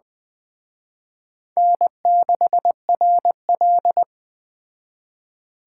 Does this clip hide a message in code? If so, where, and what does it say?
Yes: Morse code from 1.6 to 4.0 s, heard as a keyed tone, about 21 wpm, 700 Hz. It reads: N6RL